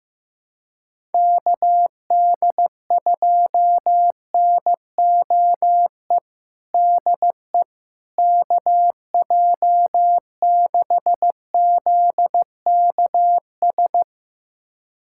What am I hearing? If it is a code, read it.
Morse code, heard as a keyed tone, about 15 wpm, 700 Hz. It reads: KD2NOE DE KJ6ZKS
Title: USA Callsign DE Callsign - 6 Characters - 15wpm